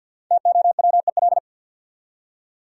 Morse code, heard as a keyed tone, about 46 wpm, 700 Hz. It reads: TOWEL
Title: Five Letter Words - 50wpm